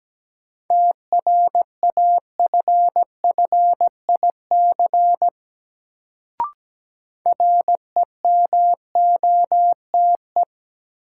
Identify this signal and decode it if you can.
Morse code, heard as a keyed tone, about 17 wpm, 700 Hz. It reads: TRAFFIC REMOTE
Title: Sentences from Top 1300 Words - New Word Drill Alt - 17wpm